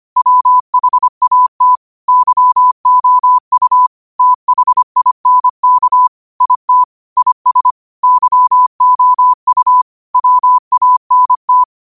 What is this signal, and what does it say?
Morse code, heard as a keyed tone, about 25 wpm, 1000 Hz. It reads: WHAT YOU THINK IT IS YOU WANT